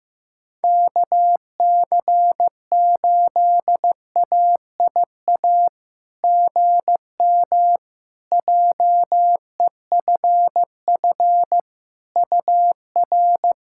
Morse code, heard as a keyed tone, about 15 wpm, 700 Hz. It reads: KC8AIA GM JEFF UR
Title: SOTA QSO Practice 15wpm